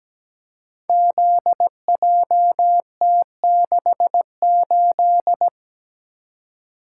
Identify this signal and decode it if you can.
Morse code, heard as a keyed tone, about 17 wpm, 700 Hz. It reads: ZJT68
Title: Code Groups - 5 Letter:Numbers - 17wpm